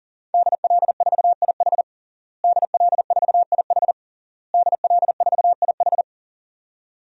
Morse code, heard as a keyed tone, about 40 wpm, 700 Hz. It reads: DL4IH DL4IH DL4IH
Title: DXCC-100 Call Signs - 5 Characters - Repeated 3x - 40wpm